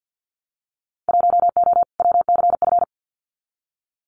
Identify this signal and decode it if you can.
Morse code, heard as a keyed tone, about 50 wpm, 700 Hz. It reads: 1O WCF
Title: Winter Field Day Exchanges 50wpm